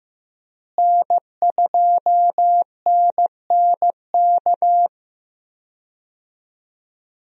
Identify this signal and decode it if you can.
Morse code, heard as a keyed tone, about 15 wpm, 700 Hz. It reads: N2NNK
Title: USA Call Signs - 5 Characters - Repeated 2x - 15wpm